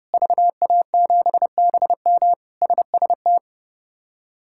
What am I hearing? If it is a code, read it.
Morse code, heard as a keyed tone, about 30 wpm, 700 Hz. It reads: VA7BM SST
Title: K1USN SST QSO Practice - 30wpm